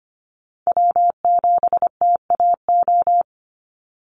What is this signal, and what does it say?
Morse code, heard as a keyed tone, about 25 wpm, 700 Hz. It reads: W7TAO
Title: NA SOTA Chaser Callsigns 25wpm